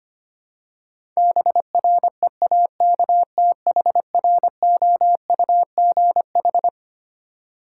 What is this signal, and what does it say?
Morse code, heard as a keyed tone, about 25 wpm, 700 Hz. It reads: BREAKTHROUGH
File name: Twelve Letter Words - 25wpm